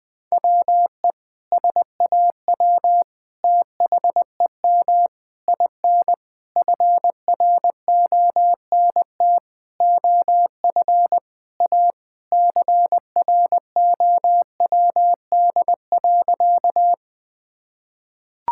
Morse code, heard as a keyed tone, about 20 wpm, 700 Hz. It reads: WE SAW THEM IN FRONT OF A CROWD.
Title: Sentences with 4-Word Phrases - 20wpm